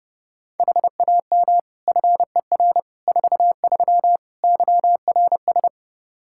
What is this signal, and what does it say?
Morse code, heard as a keyed tone, about 30 wpm, 700 Hz. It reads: HAM FER 43 YRS
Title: Practice QSO - Exchange 3 - More Info - 30wpm